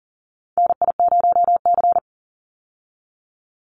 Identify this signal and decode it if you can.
Morse code, heard as a keyed tone, about 40 wpm, 700 Hz. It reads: NI0C
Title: Call Signs Repeated 2x - 40wpm